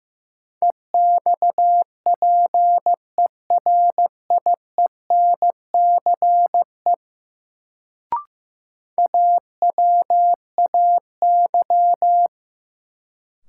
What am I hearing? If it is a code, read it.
Morse code, heard as a keyed tone, about 15 wpm, 700 Hz. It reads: EXPERIENCE AWAY
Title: Sentences from Top 600 Words - New Word Drill - 15wpm